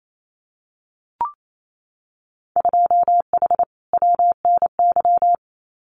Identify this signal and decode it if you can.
Morse code, heard as a keyed tone, about 28 wpm, 700 Hz. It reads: 2H WNY
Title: Winter Field Day Exchanges Encore 28wpm